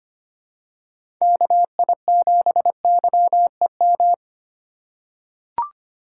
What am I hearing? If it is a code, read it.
Morse code, heard as a keyed tone, about 25 wpm, 700 Hz. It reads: KI7YEM
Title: USA Call Signs - 6 Characters - 3x Character Spacing - 25wpm